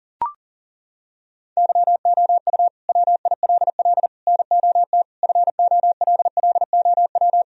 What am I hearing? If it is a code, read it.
Morse code, heard as a keyed tone, about 40 wpm, 700 Hz. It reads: YOU WILL NOT FOLLOW
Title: Sentences from Top 1000 Words - Review Part 1 - 40wpm